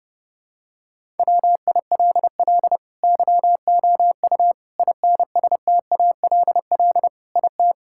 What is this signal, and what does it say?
Morse code, heard as a keyed tone, about 30 wpm, 700 Hz. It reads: WILL YOU INSTALL IT
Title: Sentences from Top 1600 Words 30wpm